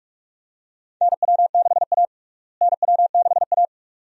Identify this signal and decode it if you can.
Morse code, heard as a keyed tone, about 46 wpm, 700 Hz. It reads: NW6A NW6A
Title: USA Call Signs - 4 Characters - Repeated 2x - 45wpm